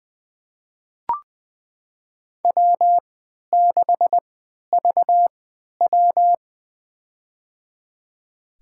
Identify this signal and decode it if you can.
Morse code, heard as a keyed tone, about 21 wpm, spaced as for 12 wpm, 700 Hz. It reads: W6VW
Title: USA Call Signs - 3x Character Spacing - 20wpm